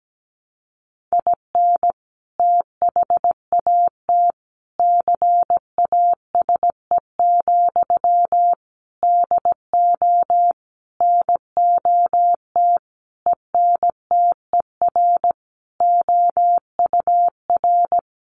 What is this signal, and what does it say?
Morse code, heard as a keyed tone, about 17 wpm, 700 Hz. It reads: IN THAT CASE, DO NOT ENTER OUR
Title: Sentences from Top 1300 Words 17wpm